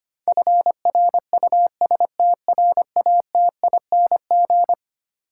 Morse code, heard as a keyed tone, about 25 wpm, 700 Hz. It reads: FRUSTRATING
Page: morse-code-ninja-practice-25wpm